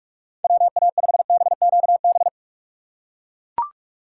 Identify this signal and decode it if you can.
Morse code, heard as a keyed tone, about 44 wpm, 700 Hz. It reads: WA5BQB
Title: USA Call Signs - 6 Characters - 6x Character Spacing - 45wpm